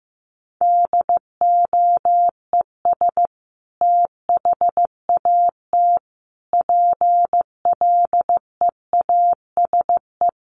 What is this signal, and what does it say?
Morse code, heard as a keyed tone, about 15 wpm, 700 Hz. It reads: DOES THAT PLEASE
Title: Sentences from Top 1500 Words 15wpm